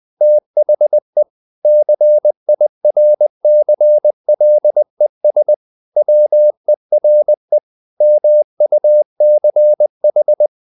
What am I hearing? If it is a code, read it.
Morse code, heard as a keyed tone, about 20 wpm, 600 Hz. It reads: THE CIRCLES WERE MUCH
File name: the old man and the sea part 2 20wpm